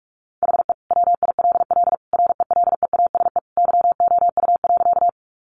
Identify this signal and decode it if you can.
Morse code, heard as a keyed tone, about 44 wpm, 700 Hz. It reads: HE WILL RELEASE YOU.
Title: Sentences from Top 1100 Words 45wpm